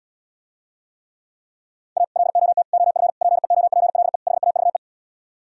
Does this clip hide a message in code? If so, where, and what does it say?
Yes: Morse code from 2.0 to 4.8 s, heard as a keyed tone, about 77 wpm, 700 Hz. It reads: A LOT OF PEOPLE HAVE